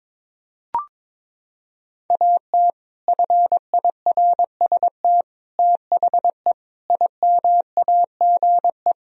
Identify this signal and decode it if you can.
Morse code, heard as a keyed tone, about 22 wpm, 700 Hz. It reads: AT FIRST THE IMAGE
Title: Sentences from Top 1000 Words - Review Part 4 - 22wpm